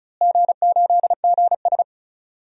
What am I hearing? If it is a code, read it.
Morse code, heard as a keyed tone, about 34 wpm, 700 Hz. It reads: G8GS
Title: DXCC-100 Call Signs - 4 Characters - Repeated 2x - 35wpm